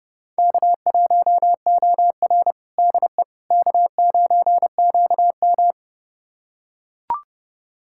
Morse code, heard as a keyed tone, about 30 wpm, 700 Hz. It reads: K1OR DE K9QM
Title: USA Callsign DE Callsign - 4 Characters - 30wpm